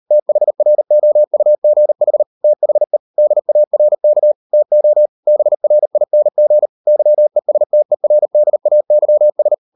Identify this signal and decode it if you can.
Morse code, heard as a keyed tone, about 38 wpm, 600 Hz. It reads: THROUGH THE DARK TO BRING YESTERDAYS